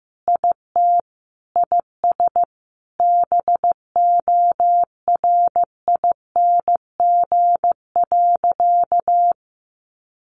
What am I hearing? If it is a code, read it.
Morse code, heard as a keyed tone, about 15 wpm, 700 Hz. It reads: IT IS BORING.
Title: Easy Phrases 15wpm